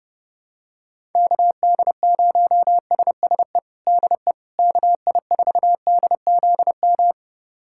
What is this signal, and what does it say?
Morse code, heard as a keyed tone, about 30 wpm, 700 Hz. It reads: KD0SSE DE KI4DZM
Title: USA Callsign DE Callsign 30wpm